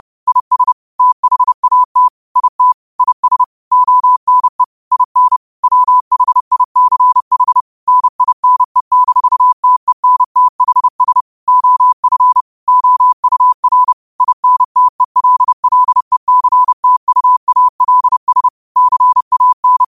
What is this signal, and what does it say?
Morse code, heard as a keyed tone, about 30 wpm, 1000 Hz. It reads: IS THAT IT IS ONE IN WHICH NINE-TENTHS OF OUR INTELLECTUALS CAN
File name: qotdfiles / 30wpm